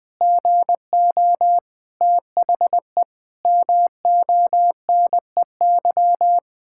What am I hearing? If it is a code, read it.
Morse code, heard as a keyed tone, about 20 wpm, 700 Hz. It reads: GO THE MONEY
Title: Pangram Challenge v2 - 20wpm